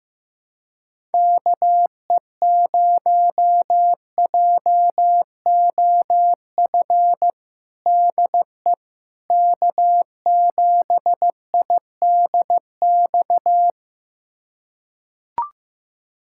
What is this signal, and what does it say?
Morse code, heard as a keyed tone, about 16 wpm, 700 Hz. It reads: KE0JOF DE K7IDX
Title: USA Callsign DE Callsign 15wpm